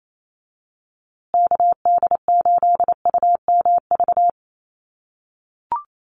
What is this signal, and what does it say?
Morse code, heard as a keyed tone, about 28 wpm, 700 Hz. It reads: KD8UMV
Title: USA Call Signs Repeated 2x 28wpm